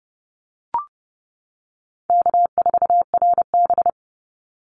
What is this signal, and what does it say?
Morse code, heard as a keyed tone, about 30 wpm, 700 Hz. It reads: K4RB
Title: Easy Call Signs - 30wpm